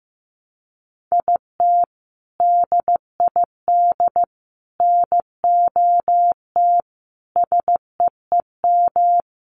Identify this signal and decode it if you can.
Morse code, heard as a keyed tone, about 15 wpm, 700 Hz. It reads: IT DID NOT SEEM
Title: Sentences from Top 1300 Words 15wpm